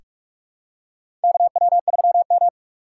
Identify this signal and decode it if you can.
Morse code, heard as a keyed tone, about 41 wpm, 700 Hz. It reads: KW3M
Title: USA Call Signs - 4 Characters - 4x Character Spacing - 45wpm